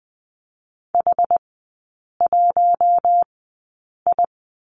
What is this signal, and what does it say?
Morse code, heard as a keyed tone, about 20 wpm, spaced as for 8 wpm, 690 Hz. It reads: H1I
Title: 060 Lesson 11e - Letter h - Three Characters 20wpm